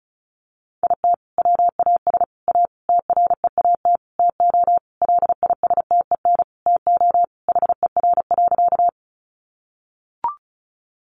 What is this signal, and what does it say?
Morse code, heard as a keyed tone, about 34 wpm, 700 Hz. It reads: IT WAS A TREAT TO LISTEN TO HER.
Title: Sentences from Top 1300 Words 35wpm